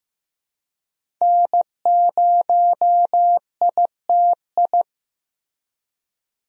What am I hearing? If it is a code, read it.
Morse code, heard as a keyed tone, about 15 wpm, 700 Hz. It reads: N0ITI
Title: USA Call Signs - 5 Characters - Repeated 2x - 15wpm